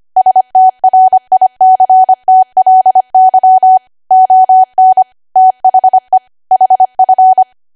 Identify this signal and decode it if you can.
Morse code, heard as a keyed tone, about 25 wpm, 750 Hz. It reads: STRICTLY ON THE HF
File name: Archive / 25 WPM / 25WPM